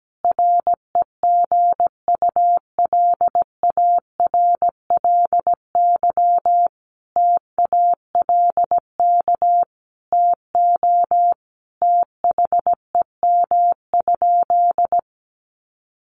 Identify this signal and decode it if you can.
Morse code, heard as a keyed tone, about 17 wpm, 700 Hz. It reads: REGULARLY TALK TO THEM?
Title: Sentences from Top 1400 Words 17wpm